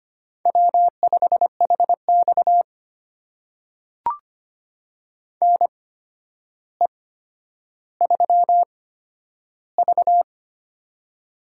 Morse code, heard as a keyed tone, about 25 wpm, spaced as for 11 wpm, 700 Hz. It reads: W5HX NE3V
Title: USA Call Signs - 4 Characters - 8x Character Spacing - 25wpm